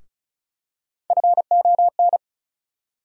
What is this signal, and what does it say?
Morse code, heard as a keyed tone, about 36 wpm, 700 Hz. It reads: FON
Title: USA Call Sign Postfixes - 3 Letters 35wpm